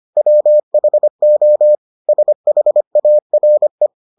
Morse code, heard as a keyed tone, about 25 wpm, 600 Hz. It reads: WHO SHARE